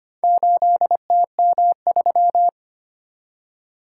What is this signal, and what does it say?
Morse code, heard as a keyed tone, about 25 wpm, 700 Hz. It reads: 8TM3
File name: practice / 25wpm / Four Letter-Number Combinations - 25wpm